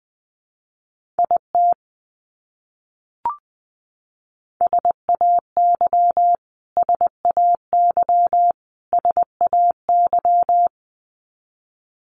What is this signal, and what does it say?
Morse code, heard as a keyed tone, about 20 wpm, 700 Hz. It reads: IT SAY SAY SAY
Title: Top 300 Words Repeated 20wpm